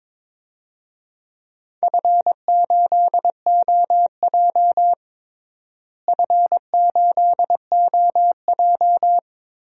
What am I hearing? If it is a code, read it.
Morse code, heard as a keyed tone, about 22 wpm, 700 Hz. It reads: F8OJ F8OJ
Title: DXCC-100 Call Signs - 4 Characters - Repeated 3x - 22wpm